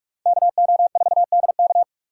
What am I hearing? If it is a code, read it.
Morse code, heard as a keyed tone, about 41 wpm, 700 Hz. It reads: KO4DK
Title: USA Call Signs - 6x Character Spacing - 45wpm